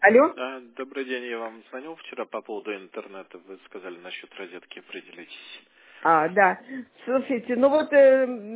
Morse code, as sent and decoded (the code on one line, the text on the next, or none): none